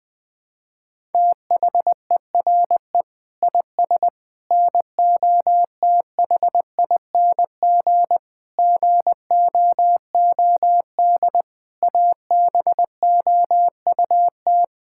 Morse code, heard as a keyed tone, about 20 wpm, 700 Hz. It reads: THERE IS NOTHING GOOD ABOUT
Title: Sentences from Top 1000 Words - Review Part 5 - 20wpm